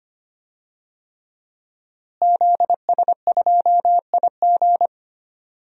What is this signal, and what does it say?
Morse code, heard as a keyed tone, about 25 wpm, 700 Hz. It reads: ZS2IG
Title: DXCC-100 Call Signs - 5 Characters - Repeated 3x - 25wpm